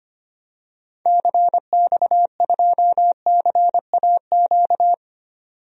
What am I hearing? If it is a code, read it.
Morse code, heard as a keyed tone, about 25 wpm, 700 Hz. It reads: CX2CAQ
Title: DXCC-100 Call Signs - 6 Characters - 25wpm